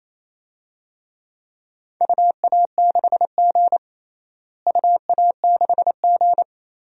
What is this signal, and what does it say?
Morse code, heard as a keyed tone, about 28 wpm, 700 Hz. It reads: UA6G UA6G
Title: DXCC-100 Call Signs - 4 Characters - Repeated 2x - 28wpm